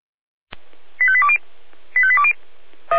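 Five-tone selective call (ZVEI), heard as quick runs of five tones, tones of about 70 ms. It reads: 85820 85820
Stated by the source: Five-tone ZVEI selective call received by SDR.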